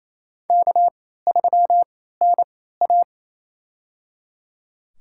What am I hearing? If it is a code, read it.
Morse code, heard as a keyed tone, about 28 wpm, spaced as for 16 wpm, 700 Hz. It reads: K3NA
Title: USA Call Signs - 4 Characters - 3x Character Spacing - 28wpm